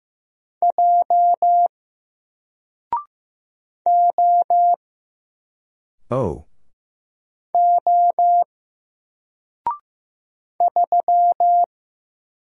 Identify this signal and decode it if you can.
Morse code, heard as a keyed tone, about 15 wpm, 700 Hz. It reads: J O O 3
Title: Single Letter-Number Alt 15wpm